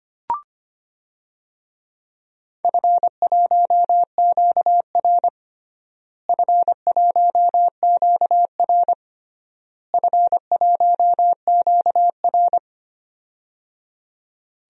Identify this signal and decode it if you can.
Morse code, heard as a keyed tone, about 25 wpm, 700 Hz. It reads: F1QR F1QR F1QR
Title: DXCC-100 Call Signs - 4 Characters - Repeated 3x - 25wpm